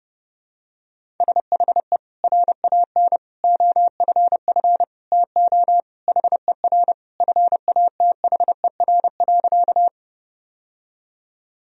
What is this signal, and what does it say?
Morse code, heard as a keyed tone, about 32 wpm, 700 Hz. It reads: SHE RAN OFF TO HER FATHER.
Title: Sentences from Top 1000 Words - Review Part 3 - 30wpm